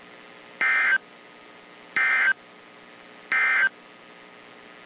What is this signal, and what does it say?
Emergency Alert System (SAME) tones, heard as harsh buzzing bursts. It is heard from 0.5 to 3.8 s.
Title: WZ2523 - NOAA Weather Radio